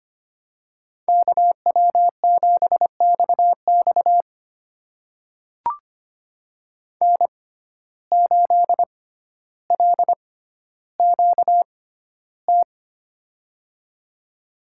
Morse code, heard as a keyed tone, about 25 wpm, spaced as for 13 wpm, 700 Hz. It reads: KW7XX N8LQT
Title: USA Call Signs - 5 Characters - 6x Character Spacing - 25wpm